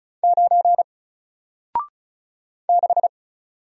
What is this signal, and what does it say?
Morse code, heard as a keyed tone, about 36 wpm, 700 Hz. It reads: 9 6
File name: Single-Letter-Number-35wpm